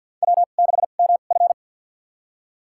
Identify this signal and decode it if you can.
Morse code, heard as a keyed tone, about 46 wpm, spaced as for 35 wpm, 700 Hz. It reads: W6MF
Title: USA Call Signs - 4 Characters - 2x Character Spacing - 50wpm